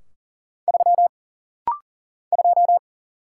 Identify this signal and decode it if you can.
Morse code, heard as a keyed tone, about 40 wpm, 700 Hz. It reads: 3 2
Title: Single Letter-Number Alt - Rapid-Fire 40wpm